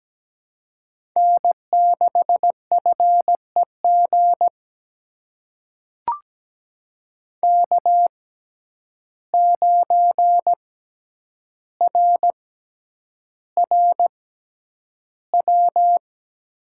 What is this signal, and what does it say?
Morse code, heard as a keyed tone, about 17 wpm, spaced as for 9 wpm, 700 Hz. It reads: N6FEG K9RRW
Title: USA Call Signs - 5 Characters - 6x Character Spacing - 17wpm